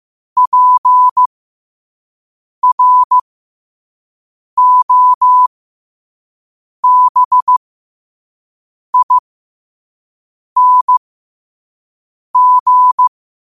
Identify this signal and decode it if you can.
Morse code, heard as a keyed tone, about 15 wpm, spaced as for 5 wpm, 1000 Hz. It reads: PROBING
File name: qotdfiles / 5wpm